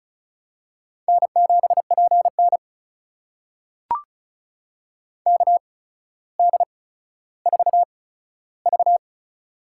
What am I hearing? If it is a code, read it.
Morse code, heard as a keyed tone, about 36 wpm, spaced as for 15 wpm, 700 Hz. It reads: N7PN KD4V
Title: USA Call Signs - 4 Characters - 8x Character Spacing - 35wpm